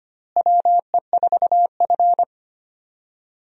Morse code, heard as a keyed tone, about 25 wpm, 700 Hz. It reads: WE4F
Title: USA Call Signs - 4 Characters - Repeated 2x - 25wpm